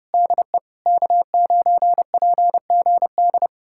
Morse code, heard as a keyed tone, about 34 wpm, 700 Hz. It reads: DE K9PGD